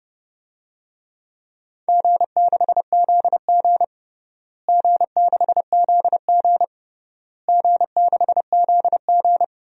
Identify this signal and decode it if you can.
Morse code, heard as a keyed tone, about 32 wpm, 700 Hz. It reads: G6ZG G6ZG G6ZG
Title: DXCC-100 Call Signs - 4 Characters - Repeated 3x - 30wpm